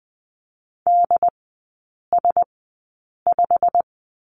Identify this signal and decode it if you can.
Morse code, heard as a keyed tone, about 20 wpm, spaced as for 8 wpm, 700 Hz. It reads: DS5
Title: 087 Lesson 15e - Number 5 - Three Characters 20wpm